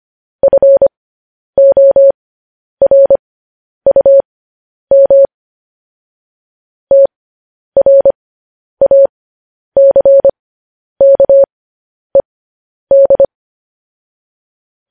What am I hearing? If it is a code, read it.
Morse code, heard as a keyed tone, about 25 wpm, spaced as for 10 wpm, 550 Hz. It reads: FORUM TRACKED